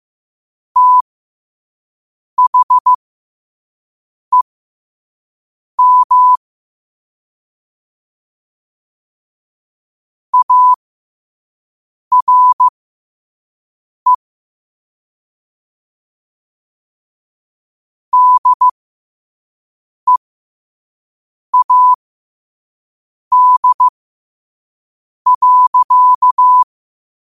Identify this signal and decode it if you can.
Morse code, heard as a keyed tone, about 15 wpm, spaced as for 5 wpm, 1000 Hz. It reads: THEM ARE DEAD.
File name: qotdfiles / 5wpm